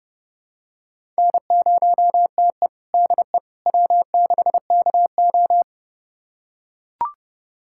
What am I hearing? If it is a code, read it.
Morse code, heard as a keyed tone, about 30 wpm, 700 Hz. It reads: N0TE DE W6KO
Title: USA Callsign DE Callsign - 4 Characters - 30wpm